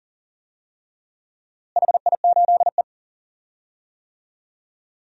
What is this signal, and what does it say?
Morse code, heard as a keyed tone, about 40 wpm, 700 Hz. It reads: HI8E
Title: DXCC-100 Call Signs - 4 Characters - Repeated 3x - 40wpm